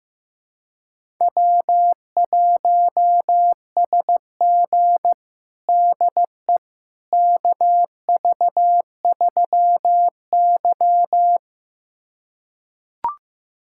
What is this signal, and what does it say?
Morse code, heard as a keyed tone, about 15 wpm, 700 Hz. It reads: W1SG DE KV3Y
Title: USA Callsign DE Callsign - 4 Characters - 15wpm